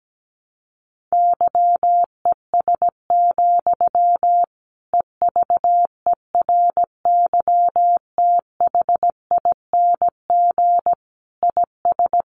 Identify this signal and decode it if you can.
Morse code, heard as a keyed tone, about 17 wpm, 700 Hz. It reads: YES, EVERYTHING IS
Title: Sentences from Top 700 Words 17wpm